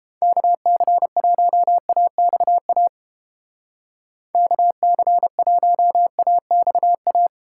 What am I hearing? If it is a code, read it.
Morse code, heard as a keyed tone, about 29 wpm, 700 Hz. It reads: KC1AXA KC1AXA